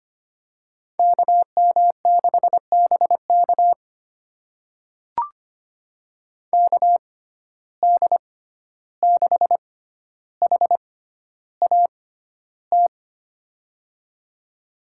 Morse code, heard as a keyed tone, about 25 wpm, spaced as for 12 wpm, 700 Hz. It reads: KM6BK KD6HAT
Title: USA Call Signs - 6x Character Spacing - 25wpm